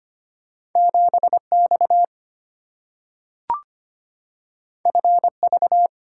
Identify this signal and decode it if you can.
Morse code, heard as a keyed tone, about 25 wpm, 700 Hz. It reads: 7X FV
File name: Two Letter-Number Combinations - 25wpm